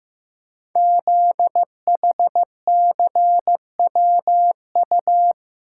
Morse code, heard as a keyed tone, about 15 wpm, 700 Hz. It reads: ZHCWU
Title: Code Groups - 5 Letters - 15wpm